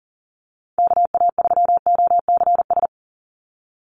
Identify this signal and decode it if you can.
Morse code, heard as a keyed tone, about 40 wpm, 700 Hz. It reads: KA3OCS
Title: Call Signs - 40wpm